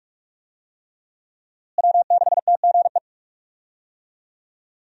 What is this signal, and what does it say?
Morse code, heard as a keyed tone, about 47 wpm, 700 Hz. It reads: W6TGE
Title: USA Call Signs - 5 Characters - Repeated 3x - 45wpm